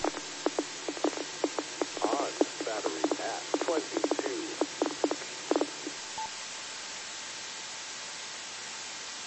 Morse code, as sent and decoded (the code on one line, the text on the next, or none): none